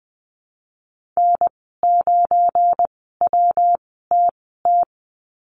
Morse code, heard as a keyed tone, about 20 wpm, spaced as for 15 wpm, 700 Hz. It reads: N9WTT
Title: 173 Lesson 28g - Number 9 - Call Signs 20wpm